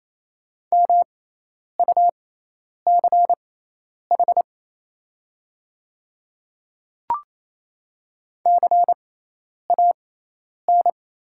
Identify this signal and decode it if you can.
Morse code, heard as a keyed tone, about 28 wpm, spaced as for 10 wpm, 700 Hz. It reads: MUCH CAN
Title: 6x Character Spacing - Top 300 Words - 28wpm